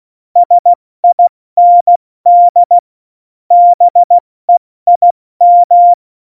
Morse code, heard as a keyed tone, about 16 wpm, 700 Hz. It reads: SIND BEIM
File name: Morsekurs / Witze / Witz 25. September 2018, 70 F80